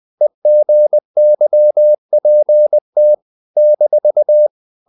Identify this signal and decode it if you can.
Morse code, heard as a keyed tone, about 20 wpm, 600 Hz. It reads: EGYPT -
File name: CW / Practice / NationalCapitals20